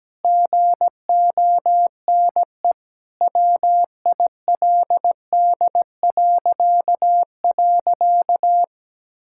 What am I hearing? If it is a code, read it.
Morse code, heard as a keyed tone, about 17 wpm, 700 Hz. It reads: GONE WILD..
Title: Pangram Challenge v2 - 17wpm